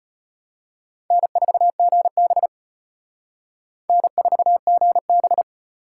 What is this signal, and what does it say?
Morse code, heard as a keyed tone, about 31 wpm, 700 Hz. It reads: N4GB N4GB
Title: Speed Racing - USA 1x2 Call Signs 25wpm